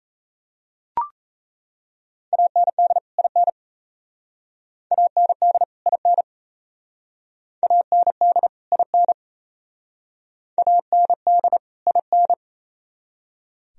Morse code, heard as a keyed tone, about 36 wpm, 700 Hz. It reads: AND IN AND IN AND IN AND IN